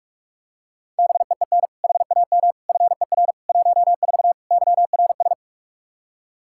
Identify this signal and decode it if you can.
Morse code, heard as a keyed tone, about 46 wpm, 700 Hz. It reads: BEEN HAM FER 14 YRS
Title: Practice QSO - Exchange 3 - More Info - 45wpm